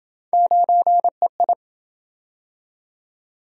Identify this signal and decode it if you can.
Morse code, heard as a keyed tone, about 27 wpm, 700 Hz. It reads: 9EI